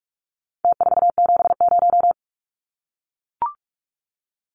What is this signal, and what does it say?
Morse code, heard as a keyed tone, about 44 wpm, 700 Hz. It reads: T470
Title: Numbers - Contest Style - 4-Digit 45wpm